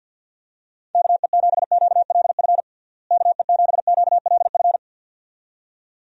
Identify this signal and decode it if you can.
Morse code, heard as a keyed tone, about 51 wpm, 700 Hz. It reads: KE7QLF KE7QLF
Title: USA Call Signs - 6 Characters - Repeated 2x - 50wpm